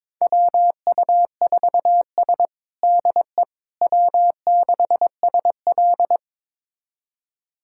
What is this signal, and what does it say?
Morse code, heard as a keyed tone, about 22 wpm, 700 Hz. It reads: WU4S DE W6SL
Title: USA Callsign DE Callsign - 4 Characters - 22wpm